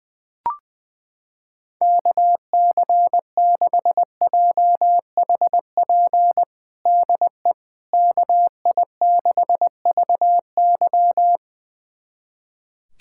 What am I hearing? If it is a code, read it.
Morse code, heard as a keyed tone, about 20 wpm, 700 Hz. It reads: KC6JHP DE KI6VY
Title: USA Callsign DE Callsign 20wpm